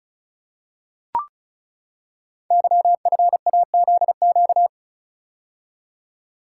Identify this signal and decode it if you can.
Morse code, heard as a keyed tone, about 36 wpm, 700 Hz. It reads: YFAZQ
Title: Code Groups - 5 Letter:Numbers - 35wpm